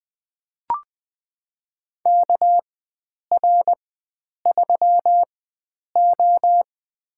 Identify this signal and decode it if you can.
Morse code, heard as a keyed tone, about 20 wpm, spaced as for 9 wpm, 700 Hz. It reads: KR3O
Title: USA Call Signs - 4 Characters - 4x Character Spacing - 20wpm